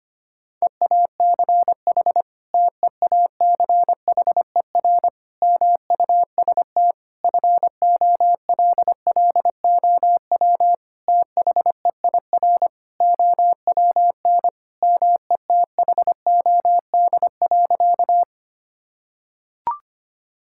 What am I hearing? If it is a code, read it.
Morse code, heard as a keyed tone, about 25 wpm, 700 Hz. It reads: EACH TEACHER MUST FOLLOW THEIR OWN METHOD.
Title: Sentences from Top 1000 Words - Review Part 6 - 25wpm